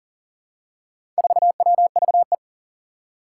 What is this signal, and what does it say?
Morse code, heard as a keyed tone, about 40 wpm, 700 Hz. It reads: 4WVE
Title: Four Challenging Letter-Number Combinations 40wpm